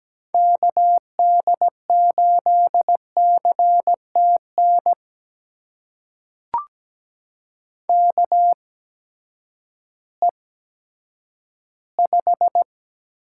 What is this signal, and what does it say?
Morse code, heard as a keyed tone, about 17 wpm, spaced as for 10 wpm, 700 Hz. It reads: KD8CTN KE5
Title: USA Call Signs - 6 Characters - 8x Character Spacing - 17wpm